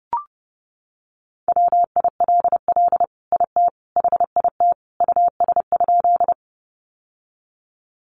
Morse code, heard as a keyed tone, about 30 wpm, 700 Hz. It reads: WILL IT HIT US?
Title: Sentences from Top 1200 Words 30wpm